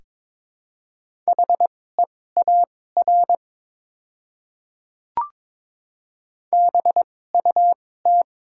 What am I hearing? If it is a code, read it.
Morse code, heard as a keyed tone, about 22 wpm, spaced as for 16 wpm, 700 Hz. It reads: HEAR BUT
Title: 2x Character Spacing - Top 500 Words - 22wpm